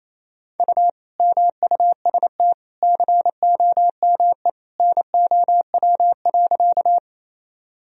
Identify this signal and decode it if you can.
Morse code, heard as a keyed tone, about 28 wpm, 700 Hz. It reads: U MUST COME NOW.
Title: Sentences from Top 200 Words and Abbreviations 28wpm